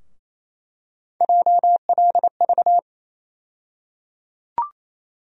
Morse code, heard as a keyed tone, about 28 wpm, 700 Hz. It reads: JLV
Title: USA Call Sign Postfixes - 3 Letters 28wpm